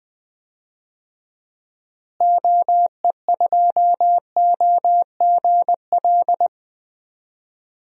Morse code, heard as a keyed tone, about 20 wpm, 700 Hz. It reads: OE2OGL
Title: DXCC-100 Call Signs - 6 Characters - Repeated 2x - 20wpm